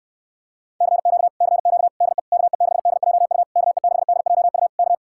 Morse code, heard as a keyed tone, about 68 wpm, 700 Hz. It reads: CQ CQ DE KE6DJU KE6DJU K